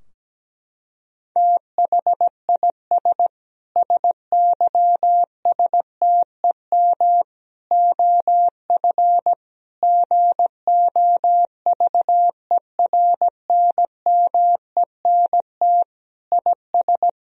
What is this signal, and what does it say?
Morse code, heard as a keyed tone, about 17 wpm, 700 Hz. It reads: THIS SYSTEM OF GOVERNMENT IS
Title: Sentences from Top 1000 Words - Review Part 3 - 17wpm